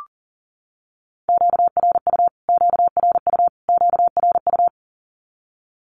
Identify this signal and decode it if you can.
Morse code, heard as a keyed tone, about 40 wpm, 700 Hz. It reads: QRU QRU QRU
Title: Q-Codes Repeated 3x 40wpm